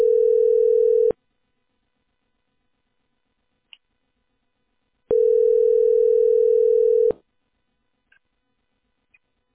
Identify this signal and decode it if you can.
Ringback tone (North America), 440 + 480 Hz.